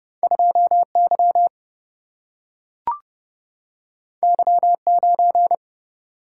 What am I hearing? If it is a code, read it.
Morse code, heard as a keyed tone, about 30 wpm, 700 Hz. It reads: 2Y Y9
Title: Single-Letter Single-Number 30wpm